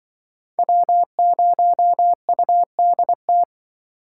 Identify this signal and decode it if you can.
Morse code, heard as a keyed tone, about 25 wpm, 700 Hz. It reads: W0UDT
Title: Speed Racing - USA 1x3 Call Signs 20wpm